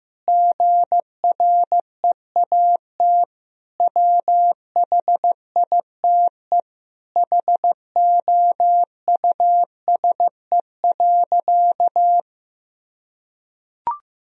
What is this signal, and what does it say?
Morse code, heard as a keyed tone, about 15 wpm, 700 Hz. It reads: GREAT WHITE HOUSE.
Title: Sentences from Top 1000 Words - Review Part 1 - 15wpm